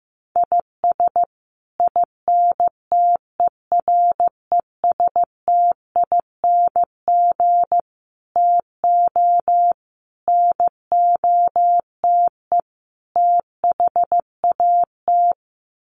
Morse code, heard as a keyed tone, about 15 wpm, 700 Hz. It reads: IS INTERESTING TO NOTE THAT
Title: Sets of 5 Words Encore2 15wpm